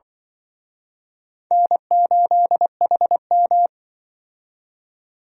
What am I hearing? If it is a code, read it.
Morse code, heard as a keyed tone, about 26 wpm, 700 Hz. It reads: N8HM